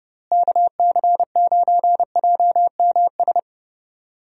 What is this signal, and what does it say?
Morse code, heard as a keyed tone, about 31 wpm, 700 Hz. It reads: KC9JMS
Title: USA Call Signs - 6 Characters - Repeated 2x - 30wpm